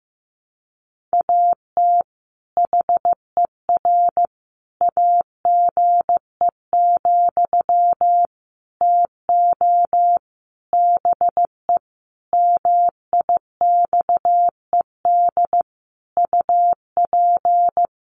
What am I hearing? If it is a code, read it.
Morse code, heard as a keyed tone, about 15 wpm, 700 Hz. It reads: AT HER AGE, TO BE MIXED UP
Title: Sentences from Top 1500 Words 15wpm